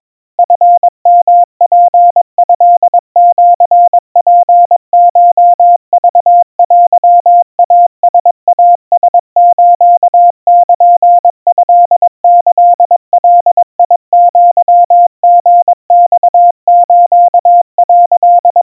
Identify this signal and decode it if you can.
Morse code (Japanese Wabun code), heard as a keyed tone, about 22 wpm, 700 Hz. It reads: チヨツトシツコクテイライラスルトキガアリマス」
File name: 80 cpm 700 Hz Crispy